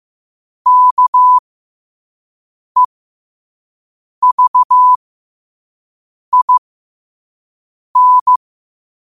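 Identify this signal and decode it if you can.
Morse code, heard as a keyed tone, about 15 wpm, spaced as for 5 wpm, 1000 Hz. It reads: KEVIN
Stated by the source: Morse code at 5 wpm.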